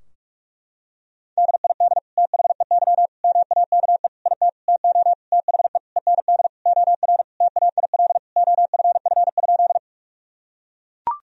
Morse code, heard as a keyed tone, about 41 wpm, 700 Hz. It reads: DID THEY MAKE IT TO THE END OR TAIL OFF?
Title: Sentences from Top 1000 Words - Review Part 3 - 45wpm